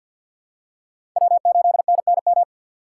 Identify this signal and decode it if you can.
Morse code, heard as a keyed tone, about 48 wpm, 700 Hz. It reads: W8NNM
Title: USA Call Signs - 5 Characters - 2x Character Spacing - 50wpm